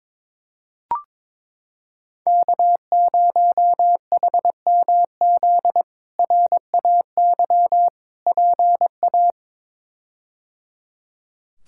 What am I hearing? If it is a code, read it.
Morse code, heard as a keyed tone, about 21 wpm, 700 Hz. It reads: K0HMZ RAY PA
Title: K1USN SST QSO Practice - 22wpm